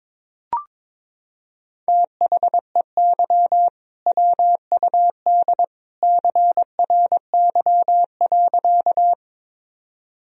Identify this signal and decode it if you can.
Morse code, heard as a keyed tone, about 22 wpm, 700 Hz. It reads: THEY WUD CRY.
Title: Sentences from Top 500 Words and Abbreviations 22wpm